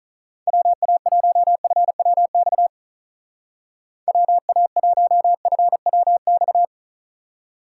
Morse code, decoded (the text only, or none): WA1FWX WA1FWX